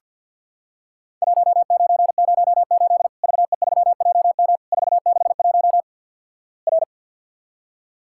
Morse code, heard as a keyed tone, about 50 wpm, 700 Hz. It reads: 1908 VE3JM 461 R
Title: Worked All Europe DX Contest - QTC Exchange - 50wpm